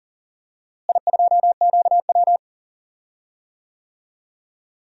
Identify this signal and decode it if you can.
Morse code, heard as a keyed tone, about 40 wpm, 700 Hz. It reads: I2QW
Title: DXCC-100 Call Signs - 4 Characters - Repeated 3x - 40wpm